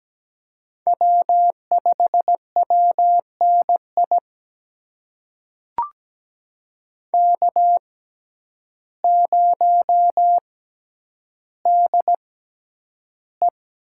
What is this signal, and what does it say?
Morse code, heard as a keyed tone, about 17 wpm, spaced as for 9 wpm, 700 Hz. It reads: W5WNI K0DE
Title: USA Call Signs - 5 Characters - 6x Character Spacing - 17wpm